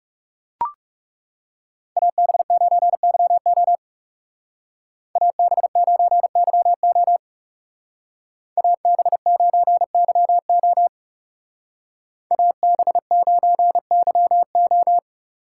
Morse code, heard as a keyed tone, about 40 wpm, 700 Hz. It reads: AB9YO AB9YO AB9YO AB9YO